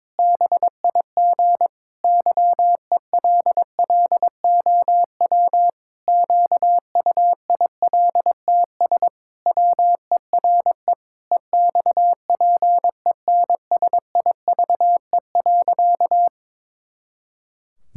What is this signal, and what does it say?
Morse code, heard as a keyed tone, about 22 wpm, 700 Hz. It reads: BIG YELLOW QUILTS WERE EXPENSIVE.
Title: Pangram Challenge v2 - 22wpm